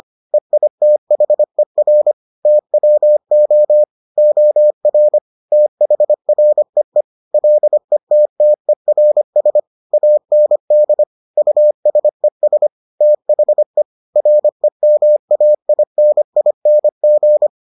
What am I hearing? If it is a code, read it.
Morse code, heard as a keyed tone, about 25 wpm, 600 Hz. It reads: EITHER TWO OR THREE LETTERS AND USES THE REMAINING